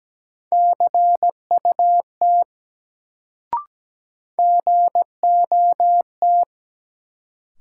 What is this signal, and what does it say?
Morse code, heard as a keyed tone, about 17 wpm, 700 Hz. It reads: CUT GOT
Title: Sentences from Top 300 Words - New Word Drill - 17wpm